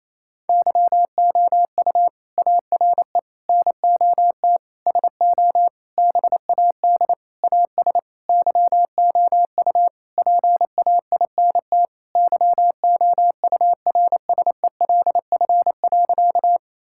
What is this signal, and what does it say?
Morse code, heard as a keyed tone, about 28 wpm, 700 Hz. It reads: YOU ARE NOT SO BAD AS YOU PAINT YOURSELF.
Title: Sentences from Top 1000 Words - Review Part 6 - 28wpm